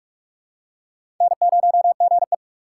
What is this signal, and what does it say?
Morse code, heard as a keyed tone, about 42 wpm, 700 Hz. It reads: N0GE
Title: USA Call Signs - 4 Characters - 2x Character Spacing - 45wpm